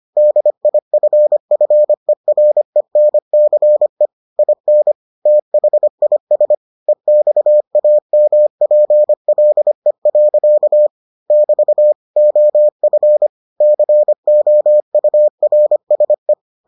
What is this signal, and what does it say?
Morse code, heard as a keyed tone, about 25 wpm, 600 Hz. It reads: DIFFERENCE IN THIS EXAMPLE. <BT> OF COURSE